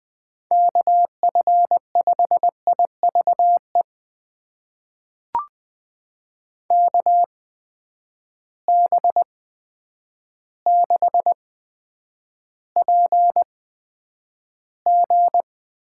Morse code, heard as a keyed tone, about 20 wpm, spaced as for 9 wpm, 700 Hz. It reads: KF5IVE KB6PG
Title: USA Call Signs - 6 Characters - 8x Character Spacing - 20wpm